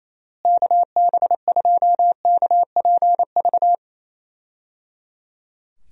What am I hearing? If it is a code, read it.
Morse code, heard as a keyed tone, about 27 wpm, 700 Hz. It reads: KB2KPV
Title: USA Call Signs - 6 Characters - Repeated 3x - 28wpm